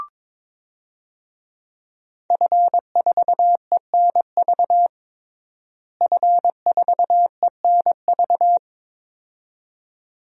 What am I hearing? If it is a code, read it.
Morse code, heard as a keyed tone, about 23 wpm, 700 Hz. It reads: F4ENV F4ENV
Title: DXCC-100 Call Signs - 5 Characters - Repeated 2x - 22wpm